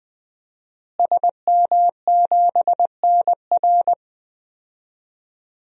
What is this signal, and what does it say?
Morse code, heard as a keyed tone, about 20 wpm, 700 Hz. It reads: SM7NR
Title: DXCC-100 Call Signs - 5 Characters - Repeated 3x - 20wpm